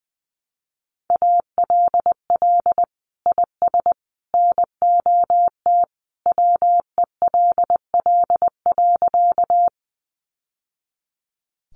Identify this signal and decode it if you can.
Morse code, heard as a keyed tone, about 20 wpm, 700 Hz. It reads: ALL IS NOT WELL.
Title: Sentences from Top 600 Words 20wpm